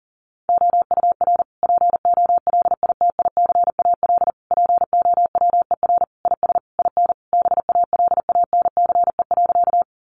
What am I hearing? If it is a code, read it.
Morse code, heard as a keyed tone, about 40 wpm, 700 Hz. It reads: OUR POLITICAL POWER IS IN BALANCE.
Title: Sentences from Top 900 Words 40wpm